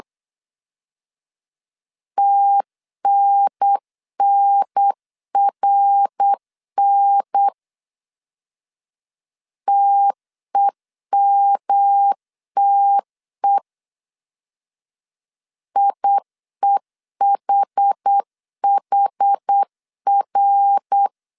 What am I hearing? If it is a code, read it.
Morse code, heard as a keyed tone, about 8 wpm, 800 Hz. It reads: TNNRN TEMTE IEHHR
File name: CW / DET / 1 3 1500C S B 8wpm 4141